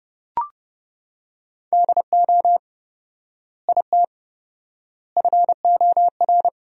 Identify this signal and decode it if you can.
Morse code, heard as a keyed tone, about 30 wpm, 700 Hz. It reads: DO IT FOR
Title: 4x Word-Spacing - Sentences from Top 200 Words - 30wpm